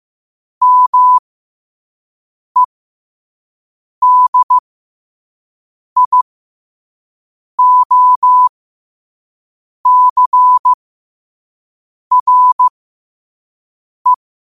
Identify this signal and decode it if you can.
Morse code, heard as a keyed tone, about 15 wpm, spaced as for 5 wpm, 1000 Hz. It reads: MEDIOCRE
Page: New quotes every day in morse code at 5 Words per minute.